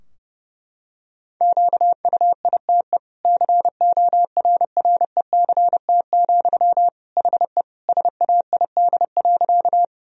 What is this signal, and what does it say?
Morse code, heard as a keyed tone, about 32 wpm, 700 Hz. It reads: QUITE CORRECT, HE SAID.
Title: Sentences from Top 1000 Words - Review Part 2 - 30wpm